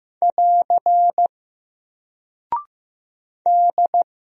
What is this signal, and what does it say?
Morse code, heard as a keyed tone, about 15 wpm, 700 Hz. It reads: <AR> D
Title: Single Character (Letter:Number:Punctuation:Prosign) Alt 15wpm